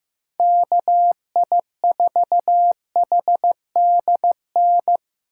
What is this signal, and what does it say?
Morse code, heard as a keyed tone, about 15 wpm, 700 Hz. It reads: KI4HDN
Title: USA Call Signs - 6 Characters - 2x Character Spacing - 15wpm